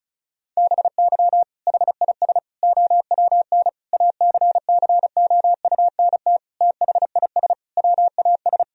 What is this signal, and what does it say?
Morse code, heard as a keyed tone, about 36 wpm, 700 Hz. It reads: BY HIS OWN ACCOUNT THIS WAS
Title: Sentences from Top 1000 Words - Review Part 1 - 35wpm